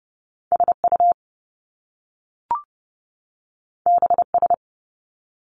Morse code, heard as a keyed tone, about 30 wpm, 700 Hz. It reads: SU BS
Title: Top Two Letter Combinations in Words 30wpm